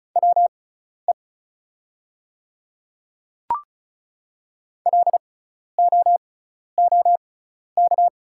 Morse code, heard as a keyed tone, about 36 wpm, spaced as for 12 wpm, 700 Hz. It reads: WE LOOK
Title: 6x Character Spacing - Top 500 Words - 35wpm